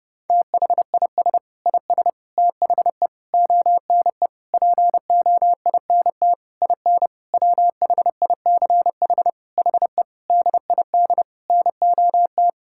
Morse code, heard as a keyed tone, about 32 wpm, 700 Hz. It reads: THIS IS THE ONE POINT IN WHICH HE DID NOT
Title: Sentences from Top 1000 Words - Review Part 5 - 30wpm